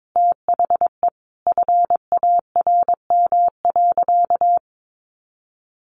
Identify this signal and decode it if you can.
Morse code, heard as a keyed tone, about 22 wpm, 700 Hz. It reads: THE FARM.
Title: Sentences from Top 1200 Words 22wpm